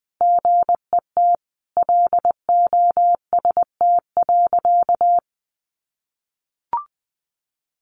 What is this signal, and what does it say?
Morse code, heard as a keyed tone, about 20 wpm, 700 Hz. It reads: GET LOST.
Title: Sentences from Top 1400 Words 20wpm